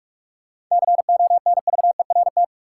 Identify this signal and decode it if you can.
Morse code, heard as a keyed tone, about 46 wpm, 700 Hz. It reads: CONVERT
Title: Seven Letter Words - 45wpm